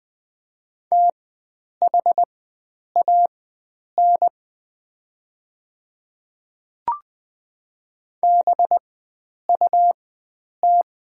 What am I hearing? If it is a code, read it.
Morse code, heard as a keyed tone, about 20 wpm, spaced as for 9 wpm, 700 Hz. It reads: THAN BUT
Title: 4x Character Spacing - Top 100 Words - 20wpm